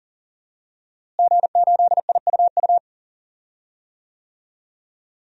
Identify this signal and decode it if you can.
Morse code, heard as a keyed tone, about 40 wpm, 700 Hz. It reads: G8IUU
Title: DXCC-100 Call Signs - 5 Characters - Repeated 2x - 40wpm